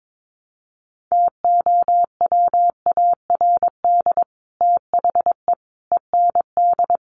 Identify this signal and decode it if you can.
Morse code, heard as a keyed tone, about 22 wpm, 700 Hz. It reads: TOWARD THE END
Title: Sets of 5 Words 22wpm